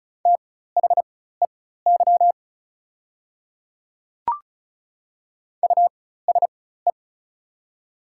Morse code, heard as a keyed tone, about 34 wpm, spaced as for 16 wpm, 700 Hz. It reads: THEY USE
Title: 4x Character Spacing - Top 200 Words - 35wpm